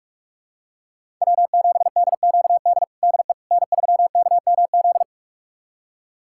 Morse code, heard as a keyed tone, about 46 wpm, 700 Hz. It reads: W7DQD DE N3KMZ
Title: USA Callsign DE Callsign - 5 Characters - 45wpm